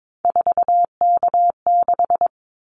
Morse code, heard as a keyed tone, about 22 wpm, 700 Hz. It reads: 4K6
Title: Callsign Prefixes for Top 100 DXCC - 22wpm